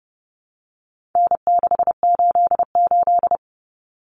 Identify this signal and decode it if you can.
Morse code, heard as a keyed tone, about 30 wpm, 700 Hz. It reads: N688
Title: Common Cut Numbers - 4-Digit 30wpm